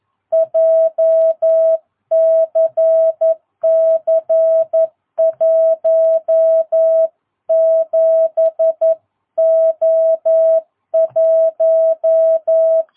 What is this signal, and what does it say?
Morse code, heard as a keyed tone, about 11 wpm, 660 Hz. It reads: JCC 17O1
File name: jh8cbh / 2022cw